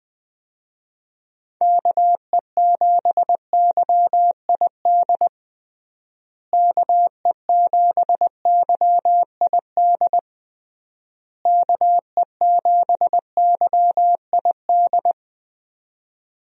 Morse code, heard as a keyed tone, about 21 wpm, 700 Hz. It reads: KE7YID KE7YID KE7YID
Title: USA Call Signs - 6 Characters - Repeated 3x - 20wpm